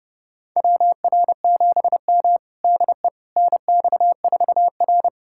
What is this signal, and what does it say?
Morse code, heard as a keyed tone, about 30 wpm, 700 Hz. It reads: WR7M DE NX4R
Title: USA Callsign DE Callsign - 4 Characters - 30wpm